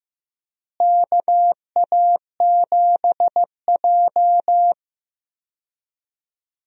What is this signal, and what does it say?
Morse code, heard as a keyed tone, about 15 wpm, 700 Hz. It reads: KA7J
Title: USA Call Signs - 4 Characters - Repeated 2x - 15wpm